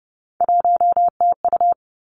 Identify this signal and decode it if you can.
Morse code, heard as a keyed tone, about 30 wpm, 700 Hz. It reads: 1TU
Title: Three Challenging Letter-Number Combinations 30wpm